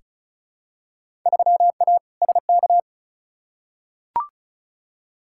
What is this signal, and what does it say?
Morse code, heard as a keyed tone, about 36 wpm, 700 Hz. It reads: 3A SK
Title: Field Day Exchanges Encore - 35wpm